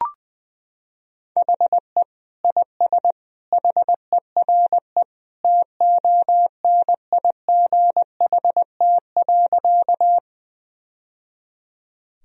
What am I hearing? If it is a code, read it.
Morse code, heard as a keyed tone, about 20 wpm, 700 Hz. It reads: HE IS HERE TONIGHT.
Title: Sentences from Top 1000 Words - Review Part 3 - 20wpm